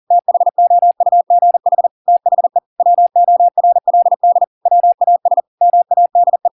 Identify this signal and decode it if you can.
Morse code, heard as a keyed tone, about 40 wpm, 700 Hz. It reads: THOUGH THE WORLD WAS MADE